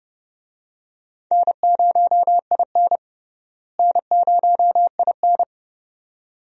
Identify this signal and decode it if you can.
Morse code, heard as a keyed tone, about 32 wpm, 700 Hz. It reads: N0IN N0IN
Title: USA Call Signs - 4 Characters - Repeated 2x - 30wpm